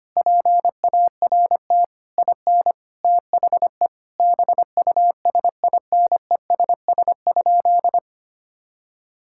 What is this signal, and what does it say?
Morse code, heard as a keyed tone, about 25 wpm, 700 Hz. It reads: PART IN THE BUSINESS?
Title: Sentences from Top 1000 Words - Review Part 6 - 25wpm